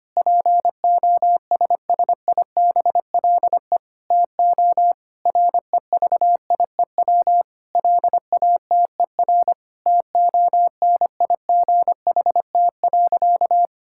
Morse code, heard as a keyed tone, about 25 wpm, 700 Hz. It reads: POSSIBLE TO REVIEW LATER TONIGHT.
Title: Sentences from Top 1000 Words - Review Part 6 - 25wpm